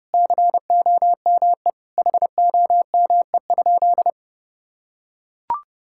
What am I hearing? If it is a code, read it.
Morse code, heard as a keyed tone, about 30 wpm, 700 Hz. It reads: COME HOME?
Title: Sentences from Top 1600 Words 30wpm